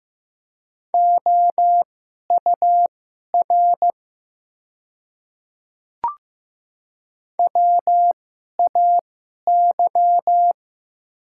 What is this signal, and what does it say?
Morse code, heard as a keyed tone, about 15 wpm, spaced as for 11 wpm, 690 Hz. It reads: OUR WAY
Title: 2x Character Spacing - Top 300 Words - 15wpm